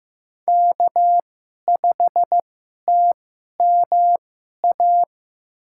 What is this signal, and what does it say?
Morse code, heard as a keyed tone, about 15 wpm, spaced as for 11 wpm, 700 Hz. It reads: K5TMA
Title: USA Call Signs - 5 Characters - 2x Character Spacing - 15wpm